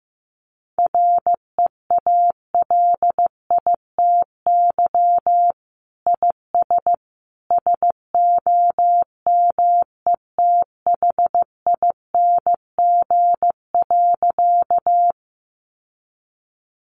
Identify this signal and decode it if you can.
Morse code, heard as a keyed tone, about 15 wpm, 700 Hz. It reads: REALITY IS SOMETHING.
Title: Sentences from Top 1200 Words 15wpm